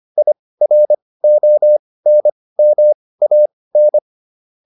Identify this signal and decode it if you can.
Morse code, heard as a keyed tone, about 25 wpm, spaced as for 18 wpm, 600 Hz. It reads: IRONMAN